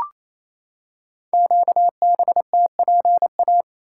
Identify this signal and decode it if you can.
Morse code, heard as a keyed tone, about 29 wpm, 700 Hz. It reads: QBTPA
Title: Code Groups - 5 Letters - 28wpm